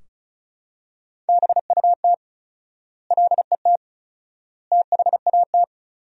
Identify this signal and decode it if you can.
Morse code, heard as a keyed tone, about 36 wpm, 700 Hz. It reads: BUT LET THAT
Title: 4x Word-Spacing - Sentences from Top 400 Words - 35wpm